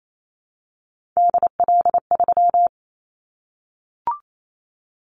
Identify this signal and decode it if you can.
Morse code, heard as a keyed tone, about 28 wpm, 700 Hz. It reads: DL3
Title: Callsign Prefixes for Top 25 DXCC - Repeated 3x - 28wpm